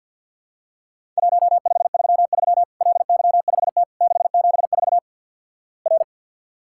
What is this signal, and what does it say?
Morse code, heard as a keyed tone, about 53 wpm, 700 Hz. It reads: 1533 LY5T 674 R
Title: Worked All Europe DX Contest - QTC Exchange - 50wpm